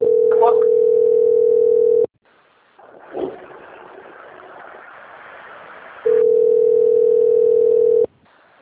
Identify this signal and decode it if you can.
Ringback tone (North America), 440 + 480 Hz.